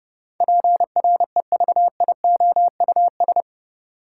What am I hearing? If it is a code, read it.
Morse code, heard as a keyed tone, about 30 wpm, 700 Hz. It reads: PREVIOUS
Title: Sentences from Top 1500 Words - New Word Drill - 30wpm